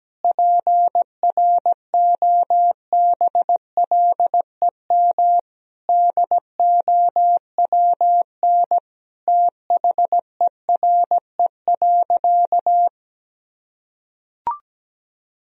Morse code, heard as a keyed tone, about 17 wpm, 700 Hz. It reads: PROBLEM DOWN THERE.
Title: Sentences from Top 1000 Words - Review Part 4 - 17wpm